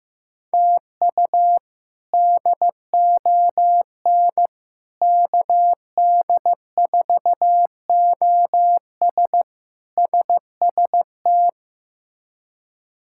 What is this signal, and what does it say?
Morse code, heard as a keyed tone, about 15 wpm, 700 Hz. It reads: TU DON KD4OS SST
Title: K1USN SST QSO Practice - 15wpm